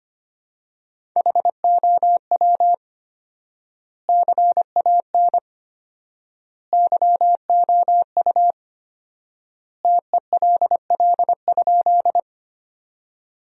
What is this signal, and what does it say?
Morse code, heard as a keyed tone, about 25 wpm, 700 Hz. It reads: HOW CAN YOU TELL?
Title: 4x Word-Spacing - Sentences from Top 200 Words - 25wpm